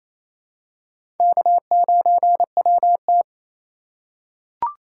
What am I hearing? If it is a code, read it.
Morse code, heard as a keyed tone, about 28 wpm, 700 Hz. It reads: K9WT
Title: USA Call Signs - 3x Character Spacing - 28wpm